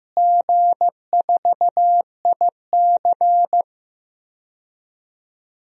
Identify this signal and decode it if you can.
Morse code, heard as a keyed tone, about 15 wpm, 700 Hz. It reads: G4IC
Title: DXCC-100 Call Signs - 4 Characters - Repeated 3x - 15wpm